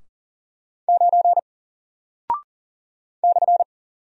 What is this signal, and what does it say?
Morse code, heard as a keyed tone, about 40 wpm, 700 Hz. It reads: 9 /
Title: Single Character (Letter:Number:Punctuation:Prosign) Alt 40wpm